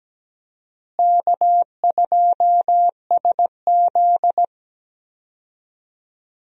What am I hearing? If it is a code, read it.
Morse code, heard as a keyed tone, about 17 wpm, 700 Hz. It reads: K2SZ
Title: USA Call Signs - 4 Characters - Repeated 2x - 17wpm